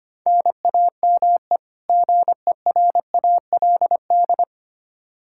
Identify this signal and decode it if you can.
Morse code, heard as a keyed tone, about 25 wpm, 700 Hz. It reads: NAME GERALD
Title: Practice QSO - Exchange 1 - Beginning - 25wpm